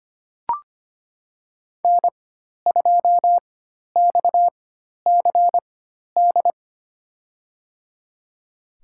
Morse code, heard as a keyed tone, about 25 wpm, spaced as for 12 wpm, 700 Hz. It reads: N2XCD
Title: USA Call Signs - 5 Characters - 4x Character Spacing - 25wpm